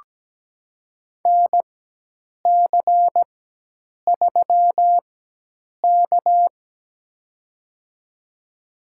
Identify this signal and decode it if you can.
Morse code, heard as a keyed tone, about 17 wpm, spaced as for 8 wpm, 700 Hz. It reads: NC3K
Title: USA Call Signs - 4 Characters - 4x Character Spacing - 17wpm